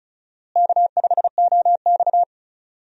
Morse code, heard as a keyed tone, about 34 wpm, 700 Hz. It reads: K5OX
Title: USA Call Signs - 4 Characters - Repeated 3x - 35wpm